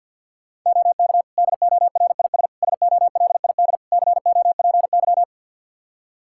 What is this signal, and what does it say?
Morse code, heard as a keyed tone, about 53 wpm, 700 Hz. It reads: OK DORIS SOLID COPY
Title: Practice QSO - Exchange 3 - More Info - 50wpm